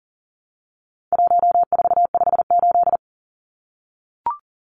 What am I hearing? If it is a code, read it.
Morse code, heard as a keyed tone, about 37 wpm, 700 Hz. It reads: 1458
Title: Numbers - 4-Digit 40wpm